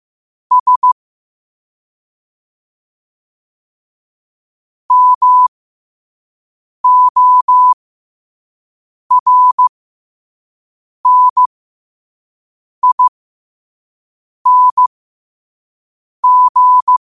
Morse code, heard as a keyed tone, about 15 wpm, spaced as for 5 wpm, 1000 Hz. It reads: S MORNING
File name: qotdfiles / 5wpm